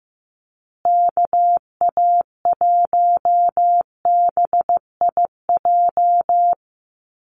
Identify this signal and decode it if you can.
Morse code, heard as a keyed tone, about 15 wpm, 700 Hz. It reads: KA1BIJ
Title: USA Call Signs Repeated 2x 15wpm